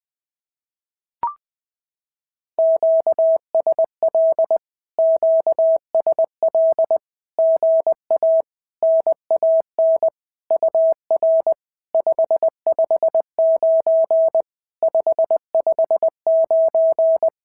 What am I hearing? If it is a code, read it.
Morse code, heard as a keyed tone, about 20 wpm, 650 Hz. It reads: QSL QSL GA NAN UR 559 559
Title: SOTA QSO Practice 20wpm